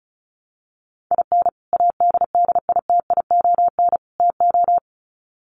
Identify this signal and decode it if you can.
Morse code, heard as a keyed tone, about 34 wpm, 700 Hz. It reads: IN ADDITION TO
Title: Sets of 3 Words 35wpm